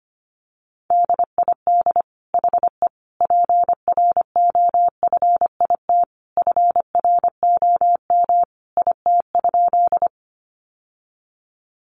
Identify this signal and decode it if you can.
Morse code, heard as a keyed tone, about 25 wpm, 700 Hz. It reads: DID HE PROFIT FROM IT?
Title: Sentences from Top 800 Words 25wpm